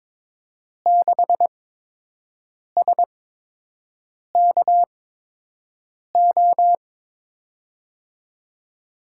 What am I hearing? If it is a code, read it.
Morse code, heard as a keyed tone, about 23 wpm, spaced as for 6 wpm, 700 Hz. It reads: 6SKO
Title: USA Call Signs - 8x Character Spacing - 22wpm